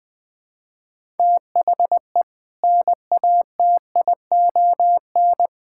Morse code, heard as a keyed tone, about 20 wpm, 700 Hz. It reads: THE NATION
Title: Build A Phrase - 5-Word Phrases - 20wpm